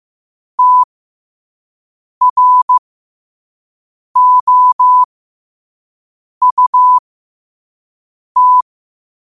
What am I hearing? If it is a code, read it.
Morse code, heard as a keyed tone, about 15 wpm, spaced as for 5 wpm, 1000 Hz. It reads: TROUT
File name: qotdfiles / 5wpm